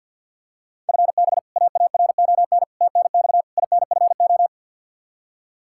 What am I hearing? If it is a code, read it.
Morse code, heard as a keyed tone, about 47 wpm, 700 Hz. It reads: FB AARON TNX INFO
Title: K1USN SST QSO Practice 50wpm